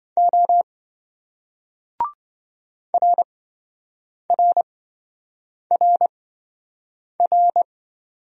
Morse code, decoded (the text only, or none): O RRRR